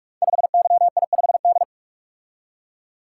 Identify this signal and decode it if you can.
Morse code, heard as a keyed tone, about 44 wpm, 700 Hz. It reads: 5YI5D
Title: Code Groups - 5 Letter:Numbers - 45wpm